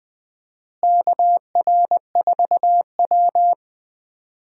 Morse code, heard as a keyed tone, about 21 wpm, 700 Hz. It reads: KR4W
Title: USA Call Signs - 4 Characters - Repeated 3x - 20wpm